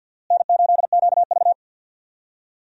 Morse code, heard as a keyed tone, about 50 wpm, 700 Hz. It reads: N8QV
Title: USA Call Signs - 4 Characters - Repeated 2x - 50wpm